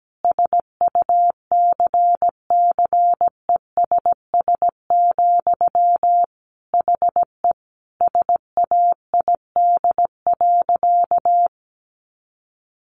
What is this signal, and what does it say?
Morse code, heard as a keyed tone, about 17 wpm, 700 Hz. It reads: SUCCESS, HE SAID.
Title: Sentences from Top 900 Words 17wpm